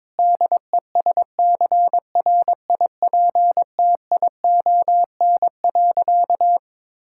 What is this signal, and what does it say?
Morse code, heard as a keyed tone, about 21 wpm, 700 Hz. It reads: DESCRIPTION.
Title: Sentences from Top 1600 Words 22wpm